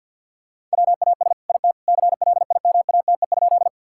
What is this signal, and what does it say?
Morse code, heard as a keyed tone, about 50 wpm, 700 Hz. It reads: WAS IT CLIMATE?
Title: Sentences from Top 1600 Words 50wpm